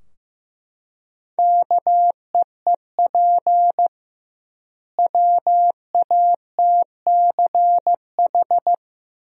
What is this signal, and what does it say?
Morse code, heard as a keyed tone, about 15 wpm, 700 Hz. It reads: KEEP WATCH
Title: 2x Word-Spacing - Sentences from Top 300 Words - 15wpm